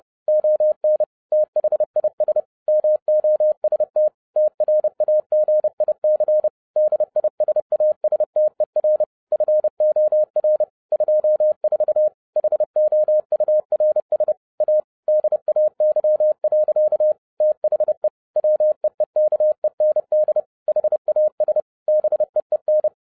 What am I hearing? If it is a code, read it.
Morse code, heard as a keyed tone, about 30 wpm, 610 Hz. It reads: ON THIS MOST TRAGIC DISASTER FOR 24 HOURS A DAY. THE WEEKEND HAS BEEN